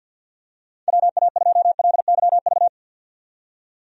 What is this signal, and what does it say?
Morse code, heard as a keyed tone, about 52 wpm, 700 Hz. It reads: WA2LYV
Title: USA Call Signs - 6 Characters - 50wpm